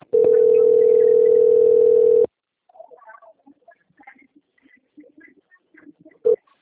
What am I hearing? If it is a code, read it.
Ringback tone (North America), 440 + 480 Hz.
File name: out- - - - - .3717